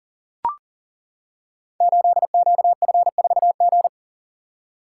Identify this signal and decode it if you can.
Morse code, heard as a keyed tone, about 42 wpm, 690 Hz. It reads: 8QF4G
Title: Code Groups - 5 Letter:Numbers - 40wpm